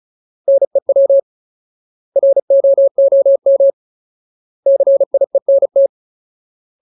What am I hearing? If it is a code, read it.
Morse code, heard as a keyed tone, about 34 wpm, 550 Hz. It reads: NEW ROOM CIENT
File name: Ultimate Word List / 35WPM Ultimate